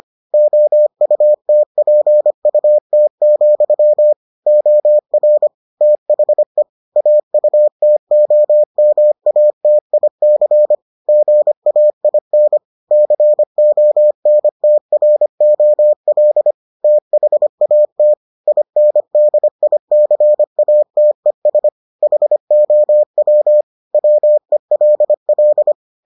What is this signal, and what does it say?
Morse code, heard as a keyed tone, about 25 wpm, 600 Hz. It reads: OUTPUT, OR THE AUTOMATIC GAIN CONTROL THAT INDICATES HOW WELL